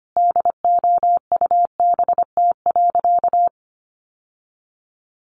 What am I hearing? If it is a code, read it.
Morse code, heard as a keyed tone, about 25 wpm, 700 Hz. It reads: DOUBT.
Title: Sentences from Top 1500 Words 25wpm